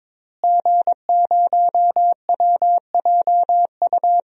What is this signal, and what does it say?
Morse code, heard as a keyed tone, about 22 wpm, 700 Hz. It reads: G0WJU
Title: DXCC-100 Call Signs - 5 Characters - Repeated 3x - 22wpm